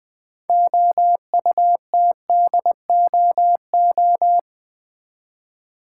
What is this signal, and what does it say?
Morse code, heard as a keyed tone, about 20 wpm, 700 Hz. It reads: OUTDOO
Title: Build A Word - 2 Letters At A Time - 8-Letter Words - 20wpm